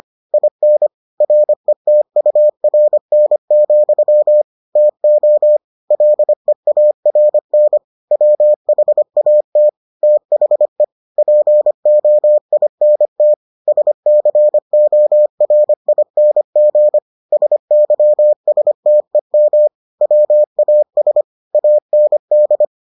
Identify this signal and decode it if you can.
Morse code, heard as a keyed tone, about 25 wpm, 600 Hz. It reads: IN RETURN, TO LEARN WHAT THE POINT SCORING SYSTEM WAS AND